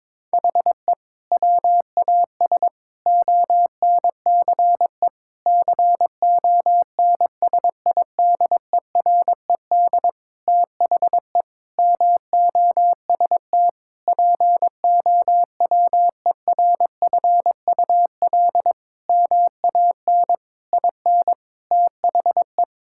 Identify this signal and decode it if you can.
Morse code, heard as a keyed tone, about 22 wpm, 700 Hz. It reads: HE WAS ONCE CONSIDERED THE MOST POWERFUL MAN IN THE
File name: Sentences with 7-Word Phrases - 22wpm